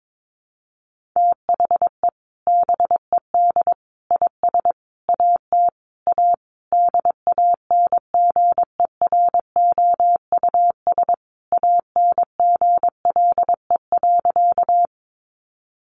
Morse code, heard as a keyed tone, about 22 wpm, 700 Hz. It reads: THE BED IS AT A DANGEROUS ANGLE.
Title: Sentences from Top 1500 Words 22wpm